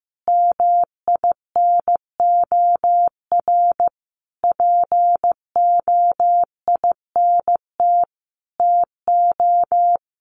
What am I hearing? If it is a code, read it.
Morse code, heard as a keyed tone, about 15 wpm, 700 Hz. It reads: MINOR POINT TO
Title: Sentences from Top 1500 Words 15wpm